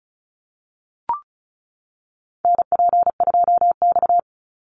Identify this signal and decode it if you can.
Morse code, heard as a keyed tone, about 34 wpm, 700 Hz. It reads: NP2X
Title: Call Signs - 35wpm